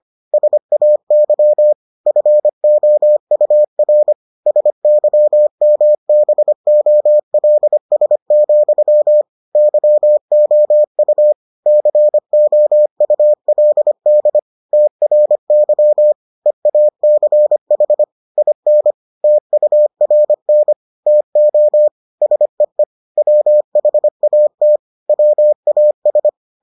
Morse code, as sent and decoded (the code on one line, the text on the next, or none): ... .- -.-- / ..-. --- ..- .-. / ... -.-- -- -... --- .-.. ... --..-- / -.-- --- ..- / -.-. --- ..- .-.. -.. / - .-. -.-- / . .- -.-. .... / .. -. / - ..- .-. -. / - --- / ... . / .-- .... .- - / .-- .- ...
SAY FOUR SYMBOLS, YOU COULD TRY EACH IN TURN TO SEE WHAT WAS